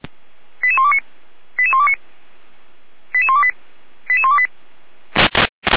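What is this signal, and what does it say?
Five-tone selective call (ZVEI), heard as quick runs of five tones, tones of about 70 ms. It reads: 88129 88129 88128 88128
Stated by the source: Five-tone ZVEI selective call received by SDR.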